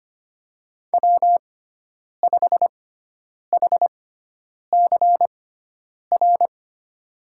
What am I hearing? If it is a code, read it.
Morse code, heard as a keyed tone, about 25 wpm, spaced as for 9 wpm, 700 Hz. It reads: W5HCR
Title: USA Call Signs - 6x Character Spacing - 25wpm